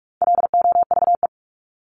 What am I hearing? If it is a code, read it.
Morse code, heard as a keyed tone, about 44 wpm, 700 Hz. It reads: LOVE
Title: Top 500 Words 45wpm